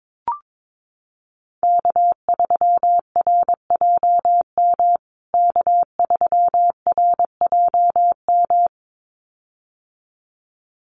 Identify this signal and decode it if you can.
Morse code, heard as a keyed tone, about 22 wpm, 700 Hz. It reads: K3RJM K3RJM
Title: USA Call Signs Repeated 2x 22wpm